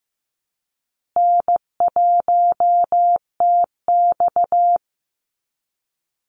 Morse code, heard as a keyed tone, about 15 wpm, 700 Hz. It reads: N1TX
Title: Easy Call Signs Repeated 3x - 15wpm